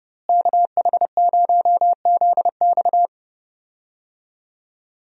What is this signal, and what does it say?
Morse code, heard as a keyed tone, about 28 wpm, 700 Hz. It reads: KH0ZX
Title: USA Call Signs - 5 Characters - Repeated 3x - 30wpm